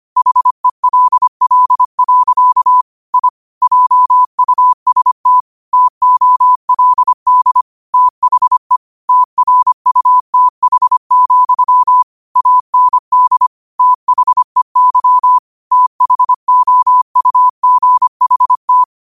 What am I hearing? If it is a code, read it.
Morse code, heard as a keyed tone, about 25 wpm, 1000 Hz. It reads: HELL. I JUST TOLD THE TRUTH, AND THEY THOUGHT
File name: qotdfiles / 25wpm